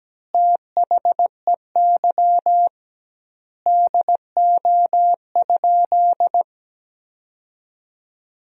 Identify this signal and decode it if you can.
Morse code, heard as a keyed tone, about 17 wpm, 700 Hz. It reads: THEY DO?
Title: 2x Word-Spacing - Sentences from Top 100 Words - 17wpm